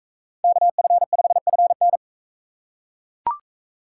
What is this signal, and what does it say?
Morse code, heard as a keyed tone, about 40 wpm, 700 Hz. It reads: KF5FN